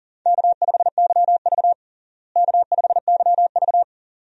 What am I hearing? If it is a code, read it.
Morse code, heard as a keyed tone, about 37 wpm, 700 Hz. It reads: K5YV K5YV
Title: USA Call Signs - 4 Characters - Repeated 2x - 40wpm